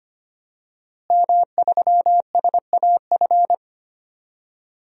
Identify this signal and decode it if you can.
Morse code, heard as a keyed tone, about 25 wpm, 700 Hz. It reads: M3SAF
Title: DXCC-100 Call Signs - 5 Characters - Repeated 3x - 25wpm